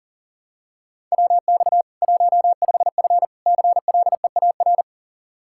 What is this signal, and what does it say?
Morse code, heard as a keyed tone, about 37 wpm, 700 Hz. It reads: WX 15F CLEAR
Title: Practice QSO - Exchange 2 - Info - 40wpm